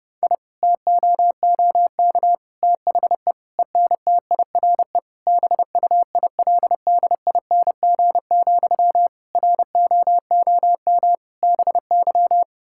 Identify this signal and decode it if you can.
Morse code, heard as a keyed tone, about 31 wpm, 700 Hz. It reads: I TOOK THE ENTIRE BUILDING, ROOM BY
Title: Sentences from Top 1000 Words - Review Part 5 - 30wpm